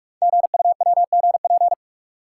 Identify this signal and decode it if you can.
Morse code, heard as a keyed tone, about 44 wpm, 700 Hz. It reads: GUWGP
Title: Code Groups - 5 Letters - 45wpm